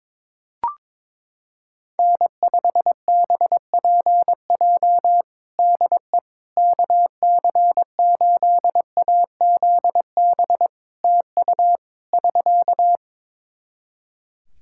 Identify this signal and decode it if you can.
Morse code, heard as a keyed tone, about 22 wpm, 700 Hz. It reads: N5BPJ DE KC8AZB TU <SK>
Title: K1USN SST QSO Practice 22wpm